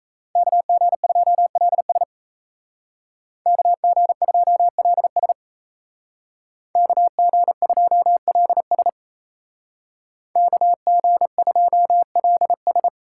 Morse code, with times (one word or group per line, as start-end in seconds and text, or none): 0.4-2.0 KG2LS
3.5-5.3 KG2LS
6.8-8.9 KG2LS
10.4-12.9 KG2LS